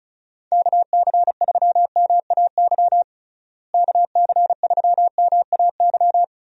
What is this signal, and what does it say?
Morse code, heard as a keyed tone, about 36 wpm, 700 Hz. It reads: KC3MAY KC3MAY
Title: USA Call Signs - 6 Characters - Repeated 3x - 35wpm